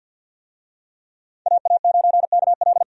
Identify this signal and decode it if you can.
Morse code, heard as a keyed tone, about 53 wpm, 700 Hz. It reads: AA9KL
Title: USA Call Signs - 5 Characters - 50wpm